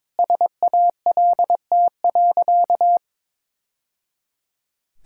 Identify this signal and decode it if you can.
Morse code, heard as a keyed tone, about 22 wpm, 690 Hz. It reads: SALT.
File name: practice / 22wpm / Easy Sentences with Hints - 22wpm